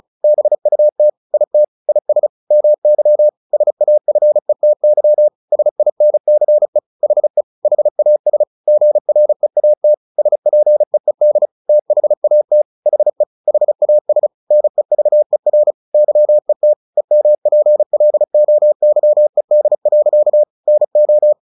Morse code, heard as a keyed tone, about 34 wpm, 600 Hz. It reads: BUT IT IS MY SAFETY SINCE HE HAS GREAT SPEED THAT HE HAS NEVER YET EMPLOYED. NO
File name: the old man and the sea 2 35wpm